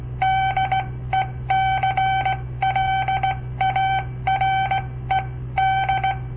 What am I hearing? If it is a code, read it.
Morse code, heard as a keyed tone, about 17 wpm, spaced as for 14 wpm, 770 Hz. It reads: DECLARED
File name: papertapes / local / Code Practice-01